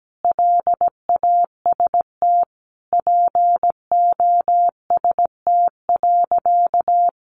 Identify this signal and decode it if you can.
Morse code, heard as a keyed tone, about 17 wpm, 700 Hz. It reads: LAST POST.
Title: Sentences from Top 1000 Words 17wpm